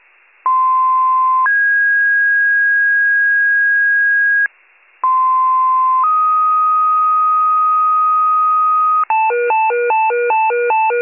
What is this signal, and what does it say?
Two-tone page, heard as a one-second tone, then a three-second tone, about 1040 then 1650 Hz, about 1040 then 1250 Hz, with noise. It is heard from 0.4 to 9.0 s.